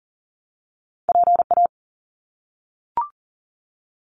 Morse code, heard as a keyed tone, about 40 wpm, 700 Hz. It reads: PA
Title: US States Repeated 3x 40wpm